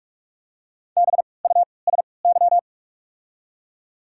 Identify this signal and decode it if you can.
Morse code, heard as a keyed tone, about 47 wpm, spaced as for 26 wpm, 700 Hz. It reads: BUSY
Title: 3x Character Spacing - Top 500 Words - 45wpm